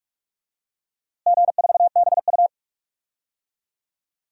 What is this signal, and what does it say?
Morse code, heard as a keyed tone, about 42 wpm, 700 Hz. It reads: G4BU
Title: DXCC-100 Call Signs - 4 Characters - Repeated 3x - 45wpm